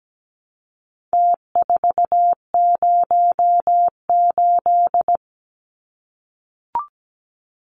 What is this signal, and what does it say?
Morse code, heard as a keyed tone, about 17 wpm, 700 Hz. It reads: T408
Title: Numbers - Contest Style - 4-Digit 17wpm